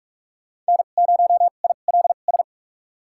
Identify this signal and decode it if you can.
Morse code, heard as a keyed tone, about 46 wpm, spaced as for 33 wpm, 700 Hz. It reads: N0ILS
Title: USA Call Signs - 2x Character Spacing - 45wpm